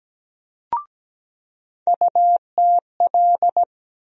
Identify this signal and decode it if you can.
Morse code, heard as a keyed tone, about 17 wpm, 700 Hz. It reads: UTL
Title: USA Call Sign Postfixes - 3 Letters 17wpm